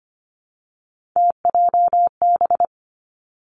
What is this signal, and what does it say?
Morse code, heard as a keyed tone, about 25 wpm, 700 Hz. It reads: TJB
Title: Three Challenging Letter-Number Combinations 25wpm